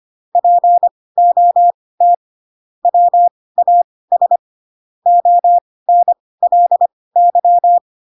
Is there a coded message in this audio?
Yes: Morse code from 0.4 to 7.8 s, heard as a keyed tone, about 25 wpm, spaced as for 18 wpm, 700 Hz. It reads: POT WAS ONLY